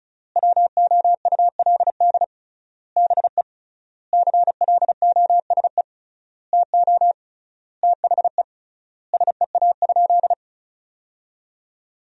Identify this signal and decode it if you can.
Morse code, heard as a keyed tone, about 34 wpm, 700 Hz. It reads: WOULD BE CLOSE TO THE SEA?
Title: 3x Word-Spacing - Sentences from Top 300 Words - 35wpm